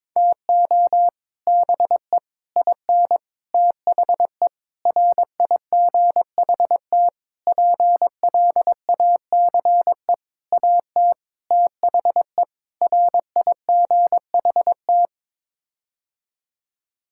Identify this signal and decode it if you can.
Morse code, heard as a keyed tone, about 22 wpm, 700 Hz. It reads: TO BE IN THE RIGHT PLACE AT THE RIGHT
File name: Sets of 9 Words - 22wpm